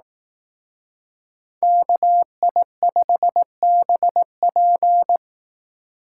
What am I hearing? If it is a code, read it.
Morse code, heard as a keyed tone, about 18 wpm, 700 Hz. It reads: KI5BP
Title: Speed Racing - USA 2x2 Call Signs 15wpm